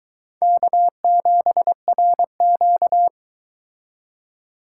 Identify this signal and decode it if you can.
Morse code, heard as a keyed tone, about 23 wpm, 700 Hz. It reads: K7RQ